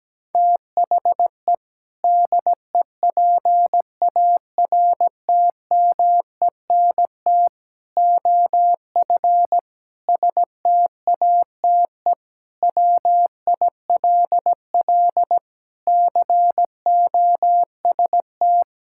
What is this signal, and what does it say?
Morse code, heard as a keyed tone, about 17 wpm, 700 Hz. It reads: THE DEPARTMENT OF STATE WILL COST
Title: Sentences from Top 1000 Words - Review Part 5 - 17wpm